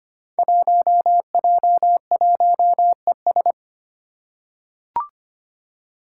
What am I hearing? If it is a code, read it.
Morse code, heard as a keyed tone, about 25 wpm, 700 Hz. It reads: 1J1ES
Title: Five Challenging Letter-Number Combinations 25wpm